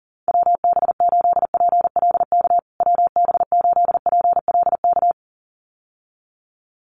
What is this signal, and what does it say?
Morse code, heard as a keyed tone, about 40 wpm, 700 Hz. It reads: WB8PLK WB8PLK
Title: USA Call Signs Repeated 2x 40wpm